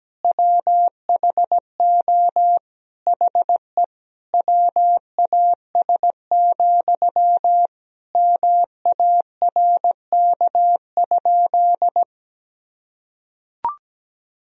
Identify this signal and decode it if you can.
Morse code, heard as a keyed tone, about 17 wpm, 700 Hz. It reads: WHO HE WAS, MARK?
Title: Sentences from Top 1000 Words - Review Part 1 - 17wpm